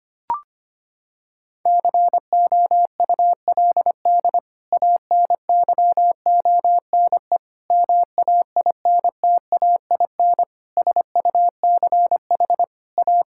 Morse code, heard as a keyed tone, about 25 wpm, 700 Hz. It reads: COULD ANYONE MAINTAIN SUCH A
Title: Sentences from Top 1000 Words - Review Part 5 - 25wpm